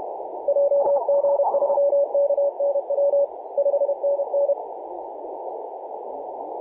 Morse code, heard as a keyed tone, about 30 wpm, 560 Hz, with noise. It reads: JG3KNW 5NN
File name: ok1ng-ti7-10-113mhz-cw